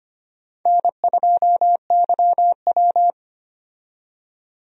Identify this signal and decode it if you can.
Morse code, heard as a keyed tone, about 25 wpm, 700 Hz. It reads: N2YW
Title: USA Call Signs - 4 Characters - 25wpm